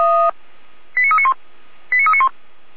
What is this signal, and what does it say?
Five-tone selective call (ZVEI), heard as quick runs of five tones, tones of about 70 ms. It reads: 89391 89391
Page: /455/SDR/ZVEI/89015/